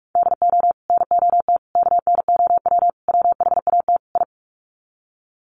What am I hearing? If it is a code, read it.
Morse code, heard as a keyed tone, about 44 wpm, 700 Hz. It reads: DO NOT KNOW WHAT I
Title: Sets of 5 Words 45wpm